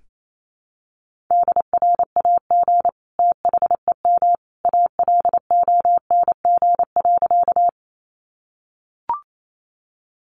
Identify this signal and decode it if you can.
Morse code, heard as a keyed tone, about 28 wpm, 700 Hz. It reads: DRAG THEM ALONG.
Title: Sentences from Top 1500 Words 28wpm